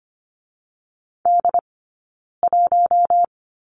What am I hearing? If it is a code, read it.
Morse code, heard as a keyed tone, about 25 wpm, 700 Hz. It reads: D 1
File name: Lesson 15d - Number 5 - Two Characters 25wpm